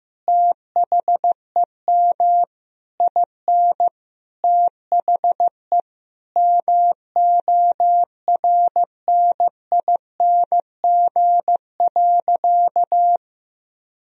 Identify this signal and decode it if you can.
Morse code, heard as a keyed tone, about 15 wpm, 700 Hz. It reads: THEM IN THE MORNING.
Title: Sentences from Top 1000 Words - Review Part 2 - 15wpm